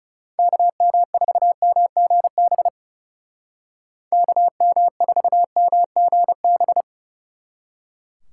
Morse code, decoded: KM4MGB KM4MGB

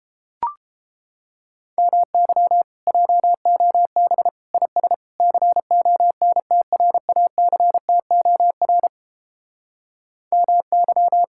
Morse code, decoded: MY JOB IS CONTRACTOR MY

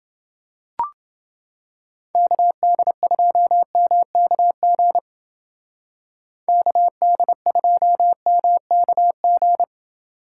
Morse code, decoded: KD2MKG KD2MKG

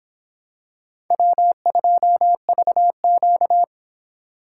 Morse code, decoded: W2VQ

